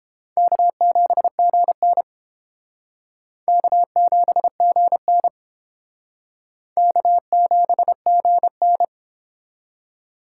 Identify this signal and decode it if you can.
Morse code, heard as a keyed tone, about 31 wpm, 700 Hz. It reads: K7GN K7GN K7GN